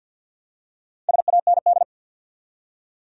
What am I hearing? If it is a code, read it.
Morse code, heard as a keyed tone, about 48 wpm, 700 Hz. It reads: SAND
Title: Four Letter Words - 50wpm